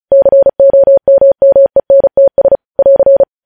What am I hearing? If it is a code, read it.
Morse code, heard as a keyed tone, about 36 wpm, 550 Hz. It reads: COMMENTS <AR>
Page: news headlines in morse code at 35 wpm